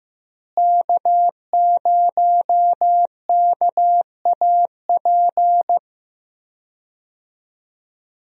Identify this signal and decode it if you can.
Morse code, heard as a keyed tone, about 16 wpm, 700 Hz. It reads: K0KAP
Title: USA Call Signs - 5 Characters - Repeated 3x - 15wpm